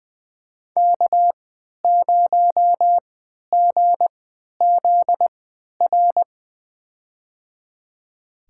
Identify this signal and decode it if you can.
Morse code, heard as a keyed tone, about 20 wpm, spaced as for 11 wpm, 700 Hz. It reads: K0GZR
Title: USA Call Signs - 3x Character Spacing - 20wpm